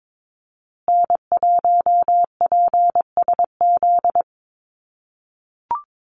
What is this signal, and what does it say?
Morse code, heard as a keyed tone, about 22 wpm, 700 Hz. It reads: N1PSZ
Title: USA Call Signs Repeated 3x 22wpm